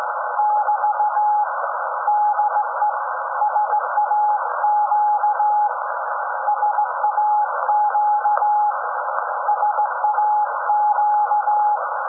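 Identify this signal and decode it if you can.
Morse code, heard as a keyed tone, about 16 wpm, 840 Hz, with noise. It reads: <BT> DE SAQ SAQ SAQ
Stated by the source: SAQ Grimeton Radio, Alexanderson 200 kW alternator on 17.2 kHz, received in Potsdam with a very strong signal on World Radio Day.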